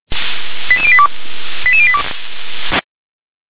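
Five-tone selective call (ZVEI), heard as quick runs of five tones, tones of about 80 ms. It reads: 88082 88082
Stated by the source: ZVEI five-tone selective call received on SDR.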